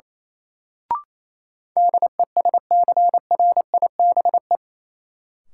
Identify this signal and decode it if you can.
Morse code, heard as a keyed tone, about 28 wpm, 700 Hz. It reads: DESCRIBE
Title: Sentences from Top 1000 Words - New Word Drill - 28wpm